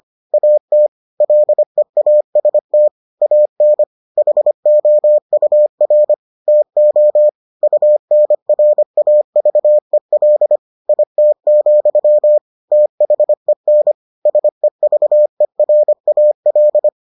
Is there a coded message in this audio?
Yes: Morse code from 0.3 to 16.9 s, heard as a keyed tone, about 25 wpm, 600 Hz. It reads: AT LEAST AN HOUR TO UNRAVEL IT, THEN SEVERAL